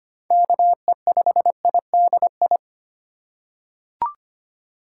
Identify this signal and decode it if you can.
Morse code, heard as a keyed tone, about 25 wpm, 700 Hz. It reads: KE5IDI
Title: USA Call Signs - 6 Characters - 4x Character Spacing - 25wpm